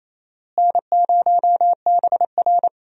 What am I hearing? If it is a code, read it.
Morse code, heard as a keyed tone, about 28 wpm, 700 Hz. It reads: N0BR
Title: USA Call Signs - 4 Characters - Repeated 3x - 28wpm